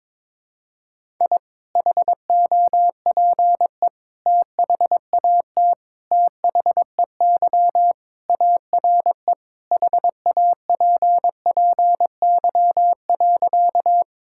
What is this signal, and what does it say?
Morse code, heard as a keyed tone, about 22 wpm, 700 Hz. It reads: I HOPE THAT THEY ARE HAPPY.
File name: practice / 22wpm / Sentences with 3-Word Phrases - 22wpm